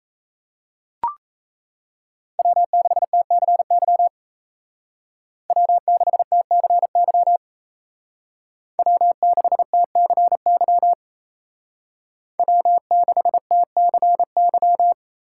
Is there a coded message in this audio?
Yes: Morse code from 2.4 to 14.9 s, heard as a keyed tone, about 36 wpm, 700 Hz. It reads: W6TCY W6TCY W6TCY W6TCY